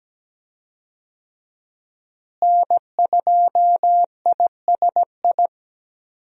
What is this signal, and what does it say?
Morse code, heard as a keyed tone, about 17 wpm, 700 Hz. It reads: N2ISI
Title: USA Call Signs - 5 Characters - Repeated 3x - 17wpm